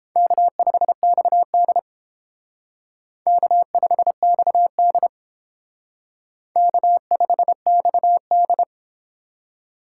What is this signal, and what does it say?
Morse code, heard as a keyed tone, about 32 wpm, 700 Hz. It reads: K5XD K5XD K5XD